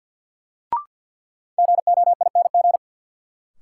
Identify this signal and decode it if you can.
Morse code, heard as a keyed tone, about 50 wpm, 700 Hz. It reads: GOING
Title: Sentences from Top 500 Words - New Word Drill - 50wpm